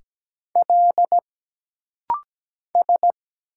Morse code, heard as a keyed tone, about 17 wpm, 700 Hz. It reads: L S
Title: Single Letter-Number - Rapid-Fire 17wpm